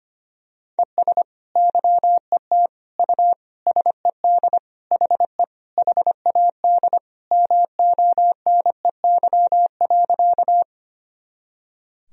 Morse code, heard as a keyed tone, about 25 wpm, 700 Hz. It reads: ES YET U SED HE HAD MONEY.
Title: Sentences from Top 500 Words and Abbreviations 25wpm